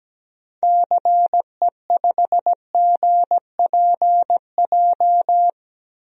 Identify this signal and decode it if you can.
Morse code, heard as a keyed tone, about 17 wpm, 700 Hz. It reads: CE5GPJ
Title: DXCC-100 Call Signs - 6 Characters - 17wpm